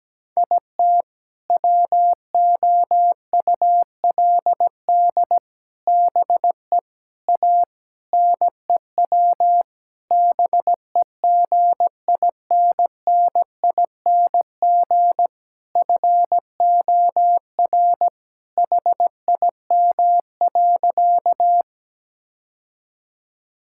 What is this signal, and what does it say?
Morse code, heard as a keyed tone, about 17 wpm, 700 Hz. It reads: IT WOULD BE A NEW BEGINNING FOR HIM.
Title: Sentences from Top 1000 Words - Review Part 5 - 17wpm